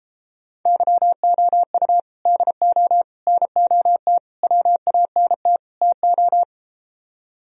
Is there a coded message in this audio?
Yes: Morse code from 0.7 to 6.4 s, heard as a keyed tone, about 33 wpm, 700 Hz. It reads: YOU DO NOT WANT TO